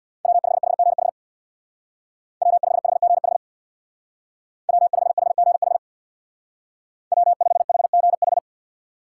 Morse code, heard as a keyed tone, about 65 wpm, 700 Hz. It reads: W5HGH W5HGH W5HGH W5HGH